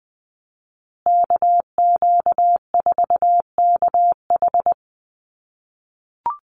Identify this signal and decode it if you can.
Morse code, heard as a keyed tone, about 19 wpm, 700 Hz. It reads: KQ4KH
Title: USA Call Signs Repeated 2x 20wpm